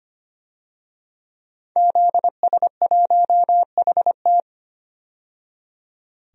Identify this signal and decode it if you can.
Morse code, heard as a keyed tone, about 25 wpm, 700 Hz. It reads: ZS1HT
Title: DXCC-100 Call Signs - 5 Characters - 25wpm